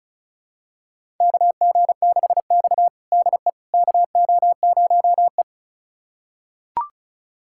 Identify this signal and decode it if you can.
Morse code, heard as a keyed tone, about 36 wpm, 700 Hz. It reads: KG6X DE KO0E
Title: USA Callsign DE Callsign - 4 Characters - 35wpm